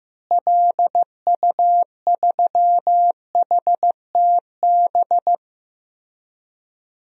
Morse code, heard as a keyed tone, about 15 wpm, 700 Hz. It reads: LU3HTB
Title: DXCC-100 Call Signs - 6 Characters - Repeated 3x - 15wpm